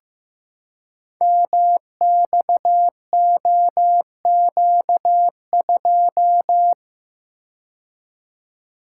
Morse code, heard as a keyed tone, about 15 wpm, 700 Hz. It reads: MXOQ2
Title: Code Groups - 5 Letter:Numbers - 15wpm